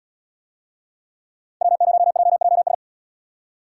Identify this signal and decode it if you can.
Morse code, heard as a keyed tone, about 78 wpm, 700 Hz. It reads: W0JJS